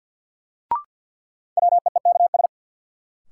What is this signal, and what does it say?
Morse code, heard as a keyed tone, about 49 wpm, 700 Hz. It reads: WEEKS
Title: Sets of 5 Words - New Word Drill - 50wpm